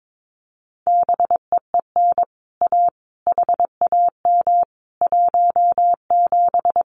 Morse code, heard as a keyed tone, about 22 wpm, 700 Hz. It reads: BEEN A HAM 17
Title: Common QSO Phrases Encore 22wpm